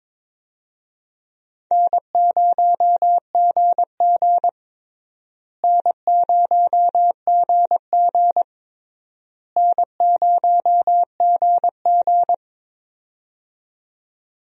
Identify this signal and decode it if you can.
Morse code, heard as a keyed tone, about 22 wpm, 700 Hz. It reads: N0GG N0GG N0GG
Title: USA Call Signs - 4 Characters - Repeated 3x - 22wpm